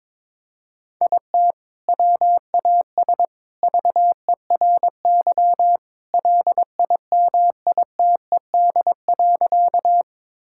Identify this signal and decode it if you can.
Morse code, heard as a keyed tone, about 22 wpm, 700 Hz. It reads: IT WAS VERY LIMITED.
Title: Sentences from Top 1000 Words - Review Part 6 - 22wpm